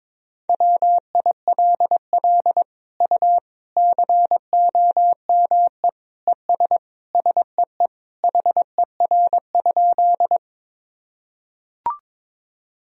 Morse code, heard as a keyed tone, about 22 wpm, 700 Hz. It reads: WILL U COME ES SEE HER?
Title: Sentences from Top 100 Words and Abbreviations 22wpm